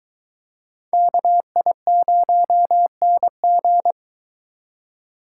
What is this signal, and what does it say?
Morse code, heard as a keyed tone, about 22 wpm, 700 Hz. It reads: KI0NG